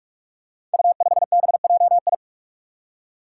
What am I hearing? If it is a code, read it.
Morse code, heard as a keyed tone, about 44 wpm, 700 Hz. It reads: U5BJI
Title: Five Challenging Letter-Number Combinations 45wpm